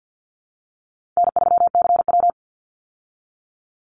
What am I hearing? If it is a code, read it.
Morse code, heard as a keyed tone, about 50 wpm, 700 Hz. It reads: N3CW
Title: Easy Call Signs - 50wpm